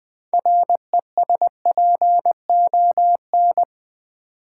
Morse code, heard as a keyed tone, about 20 wpm, 700 Hz. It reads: RESPON
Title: Build A Word - 2 Letters At A Time - 8-Letter Words - 20wpm